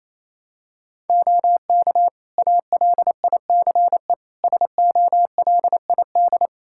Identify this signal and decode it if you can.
Morse code, heard as a keyed tone, about 28 wpm, 700 Hz. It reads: OK ALICE SOLID
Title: Practice QSO - Exchange 3 - More Info - 28wpm